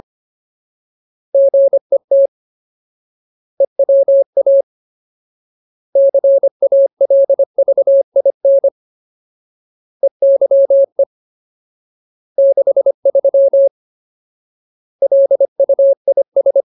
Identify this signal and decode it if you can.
Morse code, heard as a keyed tone, about 25 wpm, 550 Hz. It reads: GET EWA CALVIN EYE 63 LUIS
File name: Ultimate Word List / 25WPM Ultimate